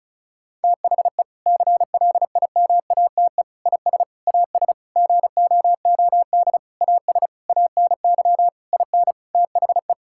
Morse code, heard as a keyed tone, about 34 wpm, 700 Hz. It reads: THE CLIMATE IS AS GOOD AS ANY IN THE
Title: Sentences from Top 1600 Words 35wpm